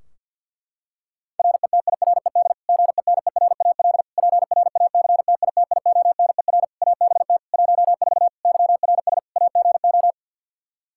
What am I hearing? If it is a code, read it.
Morse code, heard as a keyed tone, about 53 wpm, 700 Hz. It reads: RETIRED GENERAL PRACTITIONER ABT 14 YRS AGO
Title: Practice QSO - Exchange 3 - More Info - 50wpm